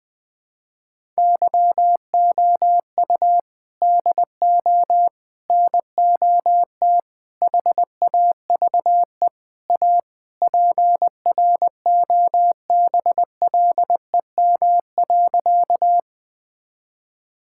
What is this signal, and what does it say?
Morse code, heard as a keyed tone, about 20 wpm, 700 Hz. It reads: YOU DO NOT HAVE A PROBLEM.
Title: Sentences with 5-Word Phrases - Spoken Beginning - 20wpm